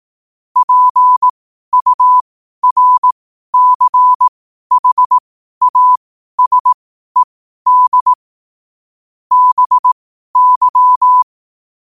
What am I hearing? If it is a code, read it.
Morse code, heard as a keyed tone, about 18 wpm, spaced as for 12 wpm, 1000 Hz. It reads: PURCHASED BY
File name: qotdfiles / 12wpm